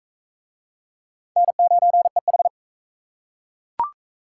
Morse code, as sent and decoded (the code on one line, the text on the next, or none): -. ----. ....
N9EH